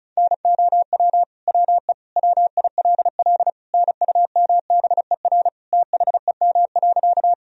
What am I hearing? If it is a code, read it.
Morse code, heard as a keyed tone, about 36 wpm, 700 Hz. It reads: NOW WE WILL NUMBER THEM.
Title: Sentences from Top 1000 Words - Review Part 1 - 35wpm